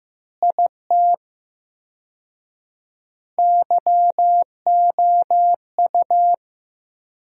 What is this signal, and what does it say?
Morse code, heard as a keyed tone, about 15 wpm, 700 Hz. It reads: IT YOU
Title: 4x Word-Spacing - Sentences from Top 200 Words - 15wpm